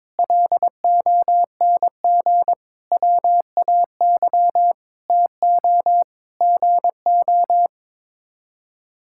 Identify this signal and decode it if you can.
Morse code, heard as a keyed tone, about 22 wpm, 700 Hz. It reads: LONG WAY TO GO
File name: practice / 22wpm / Sets of 7 Words - 22wpm